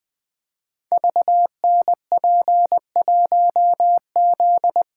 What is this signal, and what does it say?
Morse code, heard as a keyed tone, about 20 wpm, 700 Hz. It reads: VNP1Z
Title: Code Groups - 5 Letter:Numbers - 20wpm